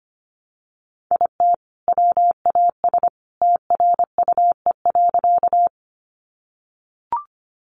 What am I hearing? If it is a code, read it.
Morse code, heard as a keyed tone, about 25 wpm, 700 Hz. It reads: IT WAS TRUE.
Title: Sentences from Top 400 Words 25wpm